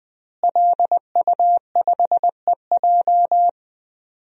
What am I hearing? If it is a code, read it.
Morse code, heard as a keyed tone, about 20 wpm, 700 Hz. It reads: LU5EJ
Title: DXCC-100 Call Signs - 5 Characters - Repeated 2x - 20wpm